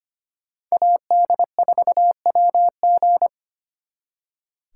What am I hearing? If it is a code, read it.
Morse code, heard as a keyed tone, about 29 wpm, 700 Hz. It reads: AD4WG
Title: Speed Racing - USA 2x2 Call Signs 25wpm